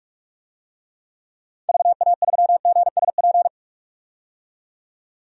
Morse code, heard as a keyed tone, about 46 wpm, 700 Hz. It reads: VA3GSP
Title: DXCC-100 Call Signs - 6 Characters - 45wpm